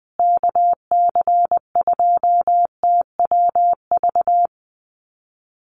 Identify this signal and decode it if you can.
Morse code, heard as a keyed tone, about 20 wpm, 700 Hz. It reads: KC2TWV
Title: USA Call Signs Repeated 3x 20wpm